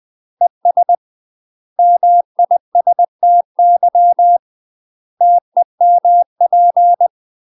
Morse code, heard as a keyed tone, about 20 wpm, 700 Hz. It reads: ES MISTY TEMP
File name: UK MASTER QSOs TRAINING / 20WPM / UK MASTER QSOs 0058